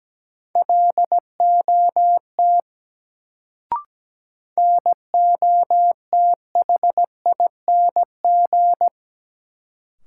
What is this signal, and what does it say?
Morse code, heard as a keyed tone, about 17 wpm, 700 Hz. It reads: LOT NOTHING
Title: Sentences from Top 500 Words - New Word Drill - 17wpm